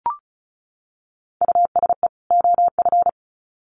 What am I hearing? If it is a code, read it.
Morse code, heard as a keyed tone, about 34 wpm, 700 Hz. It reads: USE OF
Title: Sets of 2 Words 35wpm